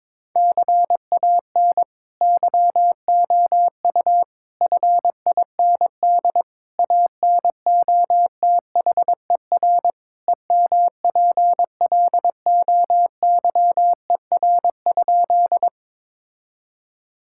Morse code, heard as a keyed tone, about 22 wpm, 700 Hz. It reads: CAN YOU FIND ANOTHER EMPLOYER?
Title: Sentences from Top 1600 Words 22wpm